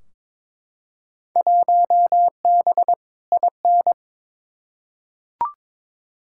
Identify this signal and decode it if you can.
Morse code, heard as a keyed tone, about 22 wpm, 700 Hz. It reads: 1B IN
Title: Field Day Exchanges - 22wpm